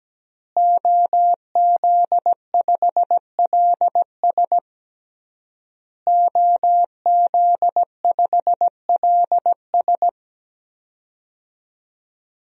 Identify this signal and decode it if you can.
Morse code, heard as a keyed tone, about 17 wpm, 700 Hz. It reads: OZ5LS OZ5LS
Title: DXCC-100 Call Signs - 5 Characters - Repeated 3x - 17wpm